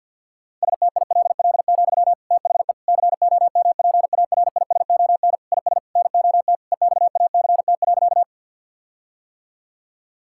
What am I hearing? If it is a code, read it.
Morse code, heard as a keyed tone, about 50 wpm, 700 Hz. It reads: STILL, THE COMPARISON IS NOT EXACT.
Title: Sentences from Top 1600 Words 50wpm